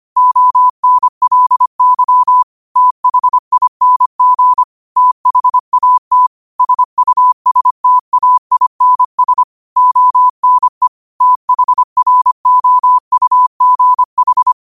Morse code, heard as a keyed tone, about 25 wpm, 1000 Hz. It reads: ONLY THING THAT SUSTAINS ONE THROUGH